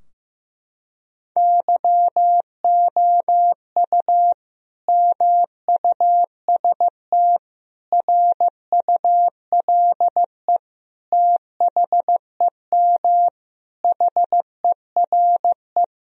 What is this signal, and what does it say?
Morse code, heard as a keyed tone, about 15 wpm, 700 Hz. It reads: YOU MUST RULE THEM HERE
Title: Sentences from Top 1000 Words - Review Part 2 - 15wpm